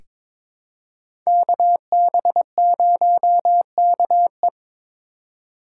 Morse code, heard as a keyed tone, about 22 wpm, 700 Hz. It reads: KB0KE
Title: USA Call Signs - 3x Character Spacing - 22wpm